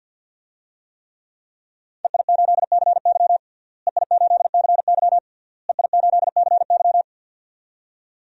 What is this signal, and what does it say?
Morse code, heard as a keyed tone, about 52 wpm, 700 Hz. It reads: EI8CY EI8CY EI8CY
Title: DXCC-100 Call Signs - 5 Characters - Repeated 3x - 50wpm